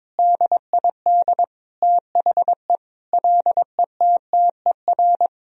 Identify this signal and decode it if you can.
Morse code, heard as a keyed tone, about 21 wpm, 700 Hz. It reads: DID THE LETTER
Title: Sentences from Top 1000 Words - Review Part 2 - 22wpm